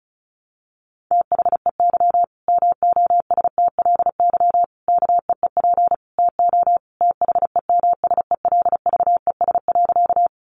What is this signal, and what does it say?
Morse code, heard as a keyed tone, about 34 wpm, 700 Hz. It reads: THEY MOSTLY KEEP TO THEMSELVES.
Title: Sentences from Top 1300 Words 35wpm